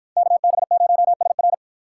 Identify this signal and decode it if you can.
Morse code, heard as a keyed tone, about 51 wpm, 700 Hz. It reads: KB0SR